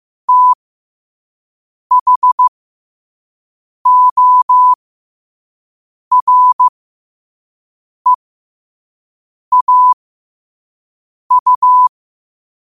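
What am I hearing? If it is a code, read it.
Morse code, heard as a keyed tone, about 15 wpm, spaced as for 5 wpm, 1000 Hz. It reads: THOREAU
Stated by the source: Morse code quote of the day at 5 wpm.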